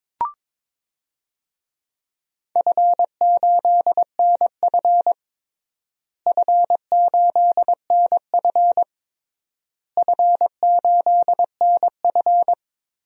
Morse code, heard as a keyed tone, about 21 wpm, 700 Hz. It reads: F8NF F8NF F8NF
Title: DXCC-100 Call Signs - 4 Characters - Repeated 3x - 22wpm